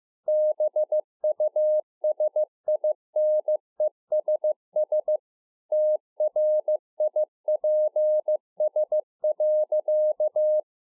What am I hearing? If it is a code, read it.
Morse code, heard as a keyed tone, about 15 wpm, 610 Hz. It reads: BUSINESS TRIPS.
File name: 15WPM